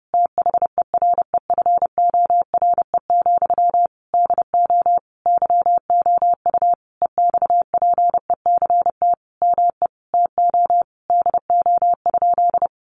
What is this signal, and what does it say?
Morse code, heard as a keyed tone, about 28 wpm, 700 Hz. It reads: THEREFORE, DO YOU EXPECT ME TO DO?
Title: Sentences from Top 1100 Words 30wpm